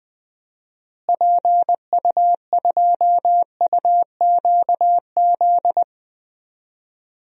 Morse code, heard as a keyed tone, about 20 wpm, 700 Hz. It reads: PU2UQZ
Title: DXCC-100 Call Signs - 6 Characters - Repeated 3x - 20wpm